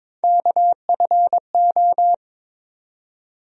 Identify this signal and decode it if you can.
Morse code, heard as a keyed tone, about 22 wpm, 700 Hz. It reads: KFO
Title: USA Call Sign Postfixes - 3 Letters 22wpm